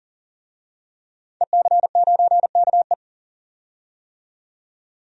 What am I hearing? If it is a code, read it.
Morse code, heard as a keyed tone, about 40 wpm, 700 Hz. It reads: EC9KE
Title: DXCC-100 Call Signs - 5 Characters - 40wpm